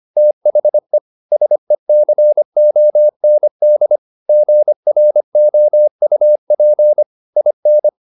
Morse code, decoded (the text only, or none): THE SECOND GROUP IN